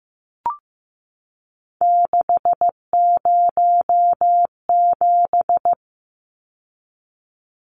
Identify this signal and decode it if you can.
Morse code, heard as a keyed tone, about 15 wpm, 700 Hz. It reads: 607
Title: Numbers - 3-Digit 15wpm